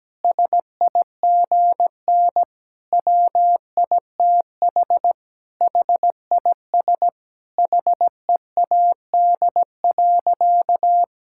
Morse code, heard as a keyed tone, about 17 wpm, 700 Hz. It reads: SIGN WITH HIS HEAD.
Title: Sentences from Top 1000 Words - Review Part 6 - 17wpm